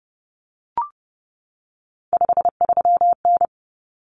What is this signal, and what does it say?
Morse code, heard as a keyed tone, about 30 wpm, 700 Hz. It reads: 53N
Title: Common Cut Numbers - 3-Digit 30wpm